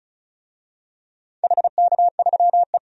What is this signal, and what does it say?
Morse code, heard as a keyed tone, about 36 wpm, 700 Hz. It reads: HK3E
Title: DXCC-100 Call Signs - 4 Characters - 35wpm